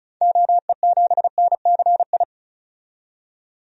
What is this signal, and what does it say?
Morse code, heard as a keyed tone, about 36 wpm, 700 Hz. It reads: OE7NCI
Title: DXCC-100 Call Signs - 6 Characters - 35wpm